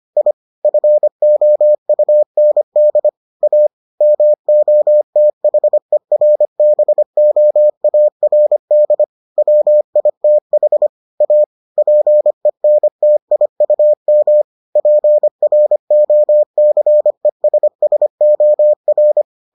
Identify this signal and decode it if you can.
Morse code, heard as a keyed tone, about 25 wpm, 600 Hz. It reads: I FOUND A MOTHERBOARD WITH A PENTIUM PROCESSOR